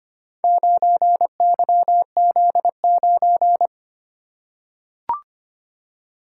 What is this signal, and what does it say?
Morse code, heard as a keyed tone, about 25 wpm, 700 Hz. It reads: 9YZ9
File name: practice / 25wpm / Four Letter-Number Combinations - 25wpm